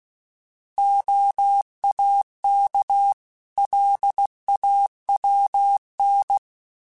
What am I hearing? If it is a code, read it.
Morse code, heard as a keyed tone, about 16 wpm, 800 Hz. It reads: OAK LAWN
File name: fromSMs / FromSMs mp3s / Cities@15